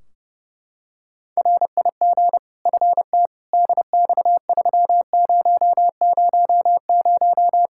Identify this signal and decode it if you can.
Morse code, heard as a keyed tone, about 30 wpm, 700 Hz. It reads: RIG FT DX3000
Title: K1USN SST QSO Practice 30wpm